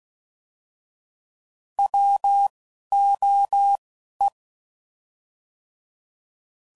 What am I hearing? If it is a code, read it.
Morse code, heard as a keyed tone, about 16 wpm, 800 Hz. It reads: W O E
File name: Prefixes / By First Letter / Ws